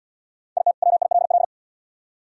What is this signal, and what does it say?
Morse code, heard as a keyed tone, about 62 wpm, 700 Hz. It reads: IT WILL
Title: Speed-Racing - Sets of 2 Words 50wpm